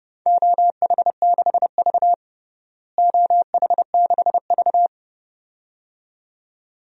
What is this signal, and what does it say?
Morse code, heard as a keyed tone, about 32 wpm, 700 Hz. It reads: OH6V OH6V
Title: DXCC-100 Call Signs - 4 Characters - Repeated 2x - 30wpm